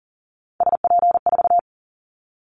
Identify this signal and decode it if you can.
Morse code, heard as a keyed tone, about 38 wpm, 700 Hz. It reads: SP4
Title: Callsign Prefixes for Top 25 DXCC - 40wpm